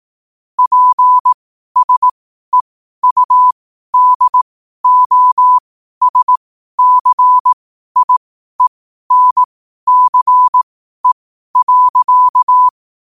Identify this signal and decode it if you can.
Morse code, heard as a keyed tone, about 18 wpm, spaced as for 12 wpm, 1000 Hz. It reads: PSEUDOSCIENCE.
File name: qotdfiles / 12wpm